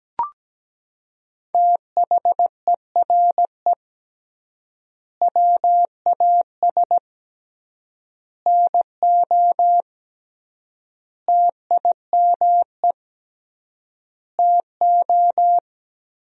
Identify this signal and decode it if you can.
Morse code, heard as a keyed tone, about 17 wpm, 700 Hz. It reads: THERE WAS NO TIME TO
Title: 3x Word-Spacing - Sentences from Top 400 Words - 17wpm